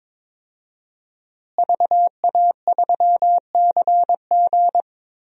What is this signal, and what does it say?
Morse code, heard as a keyed tone, about 22 wpm, 700 Hz. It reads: VA3CG
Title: DXCC-100 Call Signs - 5 Characters - Repeated 2x - 22wpm